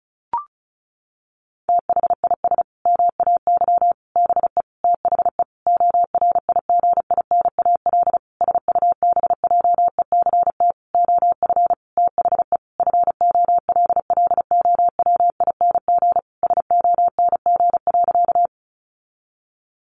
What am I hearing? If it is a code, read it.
Morse code, heard as a keyed tone, about 34 wpm, 700 Hz. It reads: THIS MAY BE THE ORIGINAL SUBJECT OF THE FOLLOWING SONG.
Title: Sentences from Top 900 Words 35wpm